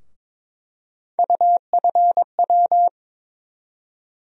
Morse code, heard as a keyed tone, about 23 wpm, 700 Hz. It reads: UFW
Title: USA Call Sign Postfixes - 3 Letters 22wpm